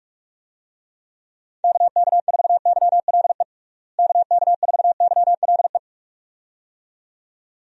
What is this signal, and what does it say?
Morse code, heard as a keyed tone, about 46 wpm, 700 Hz. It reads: KK4YLE KK4YLE
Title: USA Call Signs - 6 Characters - Repeated 2x - 45wpm